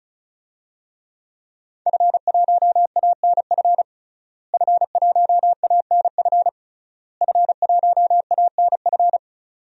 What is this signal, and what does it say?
Morse code, heard as a keyed tone, about 36 wpm, 700 Hz. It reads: F1ANF F1ANF F1ANF
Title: DXCC-100 Call Signs - 5 Characters - Repeated 3x - 35wpm